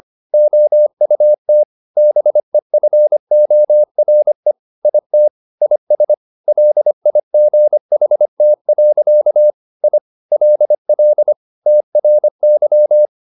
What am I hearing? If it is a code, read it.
Morse code, heard as a keyed tone, about 25 wpm, 600 Hz. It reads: OUT BEFORE IT IS LIGHT. I LL TRY
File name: the old man and the sea part 1 25wpm